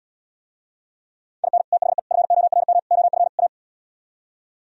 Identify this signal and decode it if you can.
Morse code, heard as a keyed tone, about 77 wpm, 700 Hz. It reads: IN THE FORM OF A